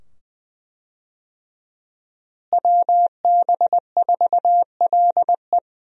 Morse code, heard as a keyed tone, about 20 wpm, 700 Hz. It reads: WB4LE
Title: USA Call Signs - 5 Characters - Repeated 2x - 20wpm